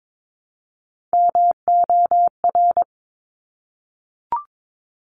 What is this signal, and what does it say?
Morse code, heard as a keyed tone, about 22 wpm, 700 Hz. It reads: MOR
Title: Top Three Letter Combinations in Words 22wpm